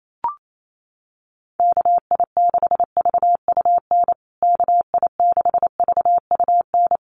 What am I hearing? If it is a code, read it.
Morse code, heard as a keyed tone, about 28 wpm, 700 Hz. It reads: KI6VUN KI6VUN
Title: USA Call Signs Repeated 2x 28wpm